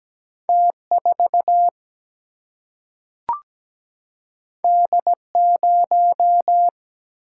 Morse code, heard as a keyed tone, about 17 wpm, 700 Hz. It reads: T4 D0
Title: Single-Letter Single-Number 17wpm